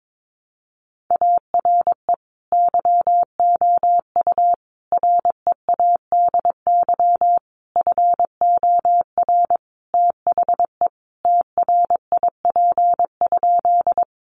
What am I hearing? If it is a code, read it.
Morse code, heard as a keyed tone, about 22 wpm, 700 Hz. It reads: ARE YOU READY FOR THE TRIP?
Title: Sentences from Top 1300 Words 22wpm